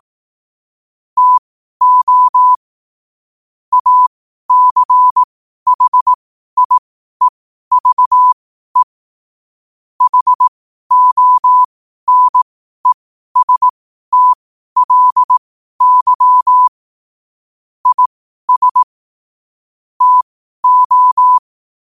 Morse code, heard as a keyed tone, about 18 wpm, spaced as for 12 wpm, 1000 Hz. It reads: TO ACHIEVE HONESTLY IS TO
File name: qotdfiles / 12wpm